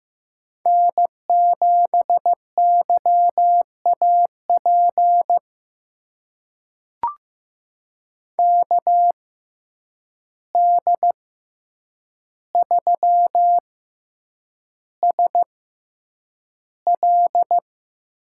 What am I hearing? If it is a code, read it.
Morse code, heard as a keyed tone, about 15 wpm, spaced as for 8 wpm, 700 Hz. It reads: N7YAP KD3SL
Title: USA Call Signs - 5 Characters - 6x Character Spacing - 15wpm